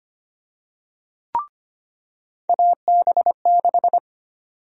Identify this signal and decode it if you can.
Morse code, heard as a keyed tone, about 25 wpm, 700 Hz. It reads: AB6
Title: USA Call Sign Prefixes - 25wpm